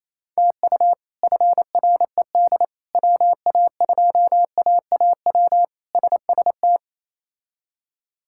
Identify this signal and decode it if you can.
Morse code, heard as a keyed tone, about 27 wpm, 700 Hz. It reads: TU FRED WA2AAW SST
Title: K1USN SST QSO Practice - 28wpm